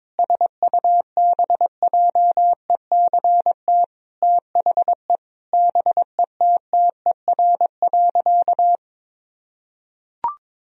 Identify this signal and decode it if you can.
Morse code, heard as a keyed tone, about 22 wpm, 700 Hz. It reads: SUBJECT THE BETTER.
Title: Sentences from Top 1000 Words - Review Part 2 - 22wpm